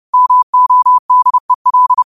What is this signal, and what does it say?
Morse code, heard as a keyed tone, about 30 wpm, 1000 Hz. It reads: MODEL
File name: qotdfiles / 30wpm